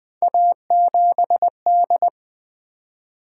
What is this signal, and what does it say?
Morse code, heard as a keyed tone, about 21 wpm, 700 Hz. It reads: A7D
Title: Three Letter-Number Combinations - 20wpm